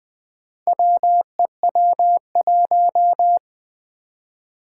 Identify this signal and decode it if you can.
Morse code, heard as a keyed tone, about 21 wpm, 700 Hz. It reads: WEW1
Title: Four Challenging Letter-Number Combinations 20wpm